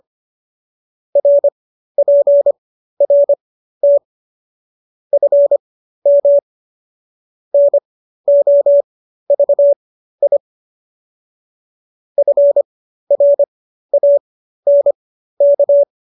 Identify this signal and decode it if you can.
Morse code, heard as a keyed tone, about 25 wpm, spaced as for 13 wpm, 570 Hz. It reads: RPRT FM NOVI FRANK